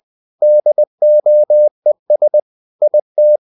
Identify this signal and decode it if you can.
Morse code, heard as a keyed tone, about 20 wpm, 600 Hz. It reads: DOES IT
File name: the old man and the sea / the old man and the sea part 1 20wpm